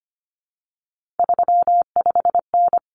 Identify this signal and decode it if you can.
Morse code, heard as a keyed tone, about 25 wpm, 700 Hz. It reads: 35N
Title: Common Cut Numbers - 3-Digit 25wpm